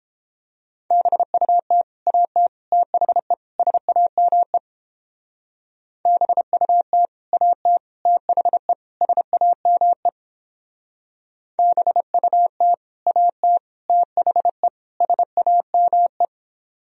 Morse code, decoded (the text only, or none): BUT AT THE SAME BUT AT THE SAME BUT AT THE SAME